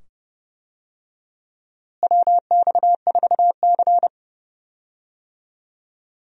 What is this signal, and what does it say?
Morse code, heard as a keyed tone, about 30 wpm, 700 Hz. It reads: WX4C
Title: USA Call Signs - 4 Characters - Repeated 2x - 30wpm